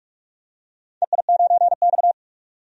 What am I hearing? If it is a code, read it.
Morse code, heard as a keyed tone, about 46 wpm, 700 Hz. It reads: EI9X
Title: DXCC-100 Call Signs - 4 Characters - 45wpm